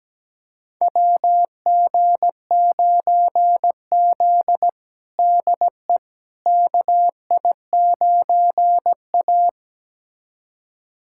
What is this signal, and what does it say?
Morse code, heard as a keyed tone, about 17 wpm, 700 Hz. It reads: WG9Z DE KI9A
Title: USA Callsign DE Callsign - 4 Characters - 17wpm